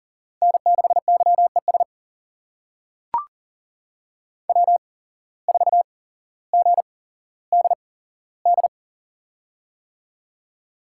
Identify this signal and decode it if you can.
Morse code, heard as a keyed tone, about 38 wpm, spaced as for 17 wpm, 700 Hz. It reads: N6YES W4GDD
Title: USA Call Signs - 5 Characters - 8x Character Spacing - 40wpm